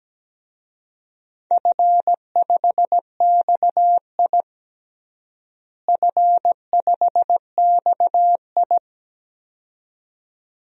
Morse code, heard as a keyed tone, about 17 wpm, 700 Hz. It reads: F5XI F5XI
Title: DXCC-100 Call Signs - 4 Characters - Repeated 2x - 17wpm